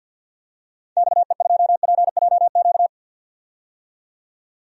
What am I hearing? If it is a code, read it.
Morse code, heard as a keyed tone, about 49 wpm, 700 Hz. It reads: XE2PJQ
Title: DXCC-100 Call Signs - 6 Characters - 50wpm